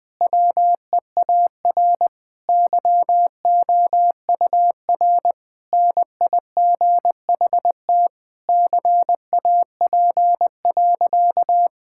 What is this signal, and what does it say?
Morse code, heard as a keyed tone, about 20 wpm, 700 Hz. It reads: WEAR YOUR NIGHT CAP.
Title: Sentences from Top 1600 Words 20wpm